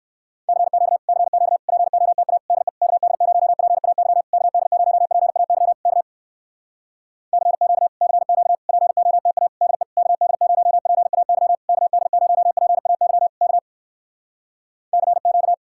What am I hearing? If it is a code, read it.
Morse code, heard as a keyed tone, about 59 wpm, 700 Hz. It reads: CQ CQ POTA DE KD0PAY KD0PAY K CQ CQ POTA DE KD0PAY KD0PAY K CQ